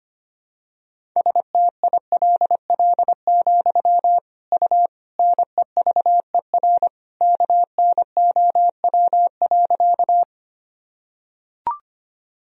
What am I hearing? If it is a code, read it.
Morse code, heard as a keyed tone, about 25 wpm, 700 Hz. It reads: STILL, U NEVER KNOW.
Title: Sentences from Top 300 Words and Abbreviations 25wpm